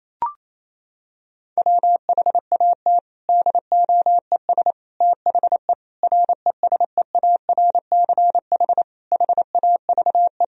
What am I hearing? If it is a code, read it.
Morse code, heard as a keyed tone, about 28 wpm, 700 Hz. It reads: WHAT DOES THE RESEARCH HAVE
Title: Sentences from Top 1000 Words - Review Part 4 - 28wpm